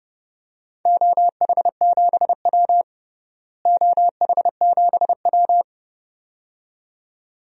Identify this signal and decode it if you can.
Morse code, heard as a keyed tone, about 30 wpm, 700 Hz. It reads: OH7W OH7W
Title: DXCC-100 Call Signs - 4 Characters - Repeated 2x - 30wpm